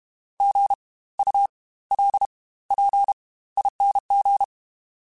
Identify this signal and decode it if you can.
Morse code, heard as a keyed tone, about 32 wpm, spaced as for 18 wpm, 800 Hz. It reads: GULPING
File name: Spaced Alphas / Spaced Alphas mp3s / Gs sp @30